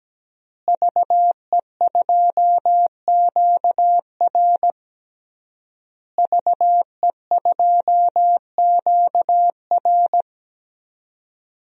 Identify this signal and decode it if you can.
Morse code, heard as a keyed tone, about 17 wpm, 700 Hz. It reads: VE2QR VE2QR
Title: DXCC-100 Call Signs - 5 Characters - Repeated 3x - 17wpm